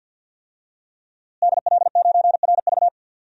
Morse code, heard as a keyed tone, about 51 wpm, 700 Hz. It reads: DL9RV
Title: DXCC-100 Call Signs - 5 Characters - Repeated 2x - 50wpm